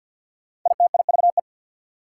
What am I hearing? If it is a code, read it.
Morse code, heard as a keyed tone, about 48 wpm, 700 Hz. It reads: ITIVE
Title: Prefixes and Suffixes 50wpm